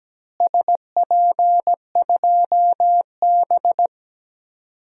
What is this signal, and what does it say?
Morse code, heard as a keyed tone, about 17 wpm, 700 Hz. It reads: SP2B
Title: DXCC-100 Call Signs - 4 Characters - Repeated 2x - 17wpm